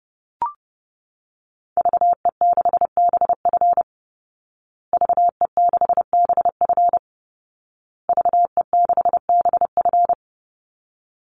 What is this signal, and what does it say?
Morse code, heard as a keyed tone, about 30 wpm, 700 Hz. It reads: VE6BF VE6BF VE6BF
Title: Call Signs Repeated 2x - 30wpm